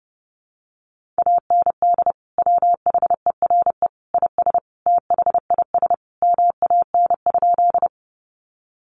Morse code, heard as a keyed tone, about 30 wpm, 700 Hz. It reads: AND WHERE IS THIS MAN?
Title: Sentences from Top 200 Words 30wpm